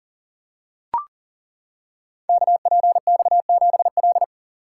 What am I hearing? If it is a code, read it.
Morse code, heard as a keyed tone, about 39 wpm, 700 Hz. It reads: KPX7L
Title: Code Groups - 5 Letter:Numbers - 40wpm